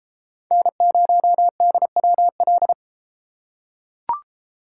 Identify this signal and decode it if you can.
Morse code, heard as a keyed tone, about 30 wpm, 700 Hz. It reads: N0DWL